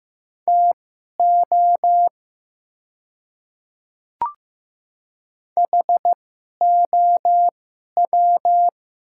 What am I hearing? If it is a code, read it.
Morse code, heard as a keyed tone, about 15 wpm, spaced as for 11 wpm, 700 Hz. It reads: TO HOW
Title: 2x Character Spacing - Top 100 Words - 15wpm